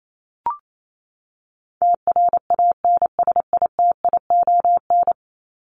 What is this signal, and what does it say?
Morse code, heard as a keyed tone, about 28 wpm, 700 Hz. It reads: TRANSITION
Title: Top 1751-2000 Words 28wpm